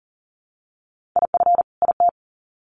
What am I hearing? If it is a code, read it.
Morse code, heard as a keyed tone, about 40 wpm, 700 Hz. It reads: IF IT
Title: Sets of 2 Words 40wpm